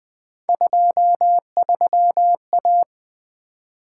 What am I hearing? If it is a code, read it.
Morse code, heard as a keyed tone, about 20 wpm, 700 Hz. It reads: 23A
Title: Three Letter-Number Combinations - 20wpm